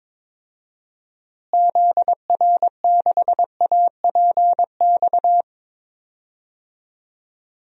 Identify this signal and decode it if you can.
Morse code, heard as a keyed tone, about 22 wpm, 700 Hz. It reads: ZR6APX
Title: DXCC-100 Call Signs - 6 Characters - Repeated 2x - 22wpm